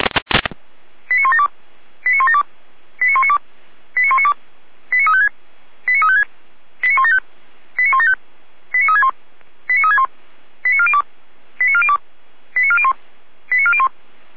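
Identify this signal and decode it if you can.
Five-tone selective call (ZVEI), heard as quick runs of five tones, tones of about 70 ms. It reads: 89172 89172 89192 89192 89356 89356 89175 89175 89361 89361 89402 89402 89401 89401
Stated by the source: Five-tone ZVEI selective call received on an SDR.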